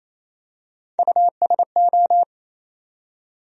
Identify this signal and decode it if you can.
Morse code, heard as a keyed tone, about 28 wpm, 700 Hz. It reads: USO
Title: USA Call Sign Postfixes - 3 Letters 28wpm